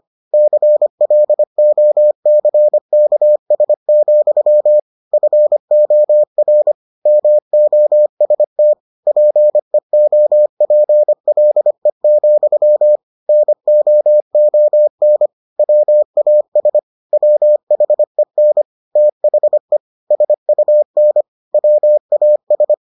Morse code, heard as a keyed tone, about 25 wpm, 600 Hz. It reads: CLOCKS, FOR MOST PEOPLE, NOON WAS WHEN THE SUN WAS